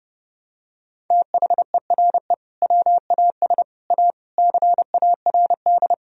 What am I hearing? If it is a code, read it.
Morse code, heard as a keyed tone, about 30 wpm, 700 Hz. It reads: THERE WAS A CARD
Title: Sentences from Top 1000 Words - Review Part 4 - 30wpm